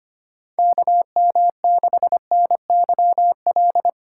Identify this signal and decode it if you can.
Morse code, heard as a keyed tone, about 25 wpm, 700 Hz. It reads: KM6NYL
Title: USA Call Signs - 6 Characters - 2x Character Spacing - 25wpm